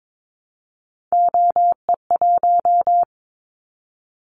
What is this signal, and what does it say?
Morse code, heard as a keyed tone, about 22 wpm, 700 Hz. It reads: OE1
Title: Callsign Prefixes for Top 25 DXCC - Repeated 3x - 22wpm